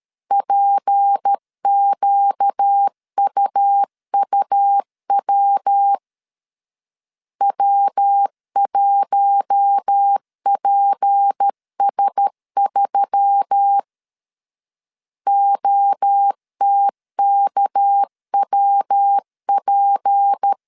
Morse code, heard as a keyed tone, about 13 wpm, 800 Hz. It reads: PQUUW W1PS3 OTKWP
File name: CW / DET / 1 6 1500C S A 12wpm 3440